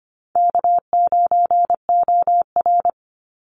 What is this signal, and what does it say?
Morse code, heard as a keyed tone, about 25 wpm, 700 Hz. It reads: K9OR
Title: Easy Call Signs Repeated 2x - 25wpm